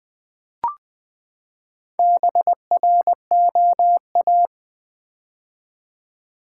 Morse code, heard as a keyed tone, about 20 wpm, 700 Hz. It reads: BROA
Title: Build A Word - 4 Letters At A Time - 11-Letter Words - 20wpm